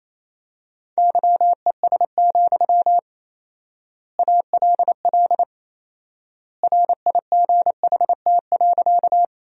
Morse code, heard as a keyed tone, about 28 wpm, 700 Hz. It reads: YES, ALL RIGHT.
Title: 4x Word-Spacing - Sentences from Top 500 Words - 28wpm